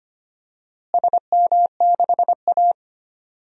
Morse code, heard as a keyed tone, about 25 wpm, 700 Hz. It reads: SM6A
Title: DXCC-100 Call Signs - 4 Characters - Repeated 3x - 25wpm